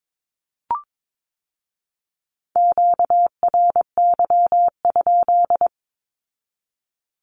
Morse code, heard as a keyed tone, about 22 wpm, 700 Hz. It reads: QRY?
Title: Q-Codes with Spoken Meaning - 22wpm